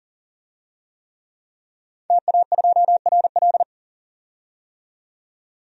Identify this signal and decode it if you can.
Morse code, heard as a keyed tone, about 37 wpm, 700 Hz. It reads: TA2RL
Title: DXCC-100 Call Signs - 5 Characters - 40wpm